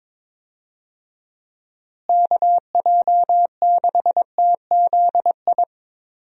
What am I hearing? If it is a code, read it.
Morse code, heard as a keyed tone, about 22 wpm, 700 Hz. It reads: KJ6TZI
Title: USA Call Signs - 6 Characters - Repeated 2x - 22wpm